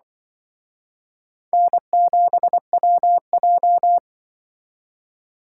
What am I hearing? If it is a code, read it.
Morse code, heard as a keyed tone, about 24 wpm, 700 Hz. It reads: N7WJ
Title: Speed Racing - USA Call Signs 20wpm